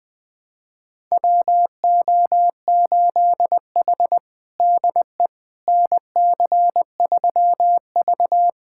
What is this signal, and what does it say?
Morse code, heard as a keyed tone, about 20 wpm, 700 Hz. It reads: WO8H DE NC3V
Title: USA Callsign DE Callsign - 4 Characters - 20wpm